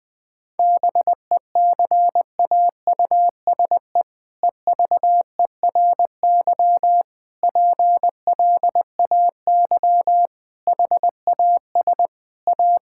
Morse code, heard as a keyed tone, about 20 wpm, 700 Hz. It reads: BECAUSE EVERY PLAY HAS A
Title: All-Age Appropriate Jokes - 20wpm